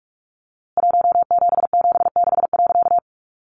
Morse code, heard as a keyed tone, about 44 wpm, 700 Hz. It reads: 1776.
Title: A Sound Of Thunder 45wpm